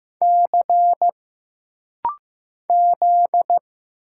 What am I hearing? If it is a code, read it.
Morse code, heard as a keyed tone, about 15 wpm, 700 Hz. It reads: C Z
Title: Single Letter-Number Alt - Rapid-Fire 15wpm